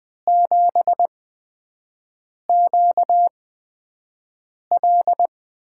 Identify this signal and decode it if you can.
Morse code, heard as a keyed tone, about 20 wpm, spaced as for 5 wpm, 700 Hz. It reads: 7QL
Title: USA Call Signs - 5 Characters - 8x Character Spacing - 20wpm